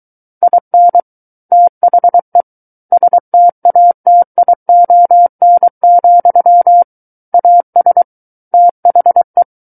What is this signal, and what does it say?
Morse code, heard as a keyed tone, about 23 wpm, 700 Hz. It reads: IN THE STATION, AS THE
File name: Book One / 14-Chapter 14 - 23 WPM